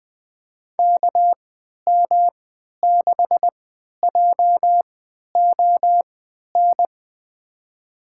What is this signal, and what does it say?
Morse code, heard as a keyed tone, about 20 wpm, spaced as for 11 wpm, 700 Hz. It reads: KM6JON
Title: USA Call Signs - 6 Characters - 3x Character Spacing - 20wpm